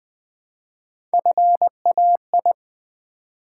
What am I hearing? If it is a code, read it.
Morse code, heard as a keyed tone, about 20 wpm, 700 Hz. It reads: FAI
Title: Build A Word - 1 Letter At A Time - 3 to 5-Letter Words - 20wpm